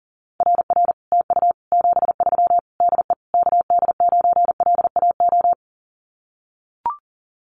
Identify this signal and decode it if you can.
Morse code, heard as a keyed tone, about 37 wpm, 700 Hz. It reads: RR TU 73 DE KD9LAO
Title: Common QSO Phrases Encore 40wpm